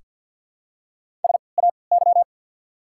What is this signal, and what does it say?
Morse code, heard as a keyed tone, about 48 wpm, spaced as for 28 wpm, 700 Hz. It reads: SAY
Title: 3x Character Spacing - Top 500 Words - 50wpm